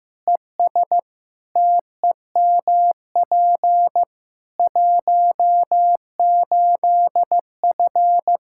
Morse code, heard as a keyed tone, about 16 wpm, 700 Hz. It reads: ES TEMP 18F
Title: Practice QSO - Exchange 2 - Info - 15wpm